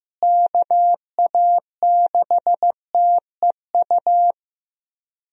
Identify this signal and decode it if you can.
Morse code, heard as a keyed tone, about 15 wpm, 700 Hz. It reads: KA6TEU
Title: USA Call Signs - 6 Characters - Repeated 2x - 15wpm